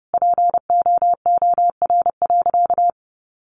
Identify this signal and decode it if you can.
Morse code, heard as a keyed tone, about 30 wpm, 700 Hz. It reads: POOR.
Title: Sentences from Top 1000 Words 30wpm